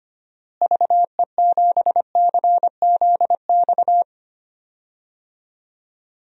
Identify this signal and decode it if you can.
Morse code, heard as a keyed tone, about 25 wpm, 700 Hz. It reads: VE7CZX
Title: DXCC-100 Call Signs - 6 Characters - Repeated 2x - 25wpm